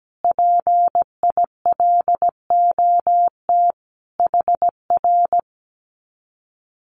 Morse code, heard as a keyed tone, about 17 wpm, 700 Hz. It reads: PILOT HR
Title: Common QSO Phrases Encore 17wpm